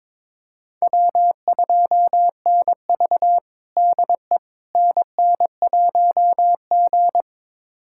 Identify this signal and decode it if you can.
Morse code, heard as a keyed tone, about 22 wpm, 700 Hz. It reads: W2NV DE NN1G
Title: USA Callsign DE Callsign - 4 Characters - 22wpm